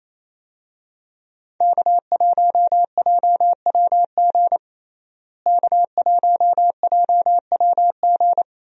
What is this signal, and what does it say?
Morse code, heard as a keyed tone, about 28 wpm, 700 Hz. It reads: K1JWG K1JWG
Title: USA Call Signs - 5 Characters - Repeated 3x - 28wpm